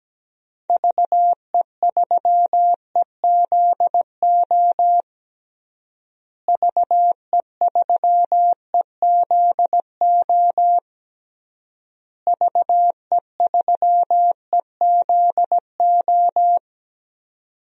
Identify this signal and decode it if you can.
Morse code, heard as a keyed tone, about 17 wpm, 700 Hz. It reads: VE3EZO VE3EZO VE3EZO
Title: DXCC-100 Call Signs - 6 Characters - Repeated 3x - 17wpm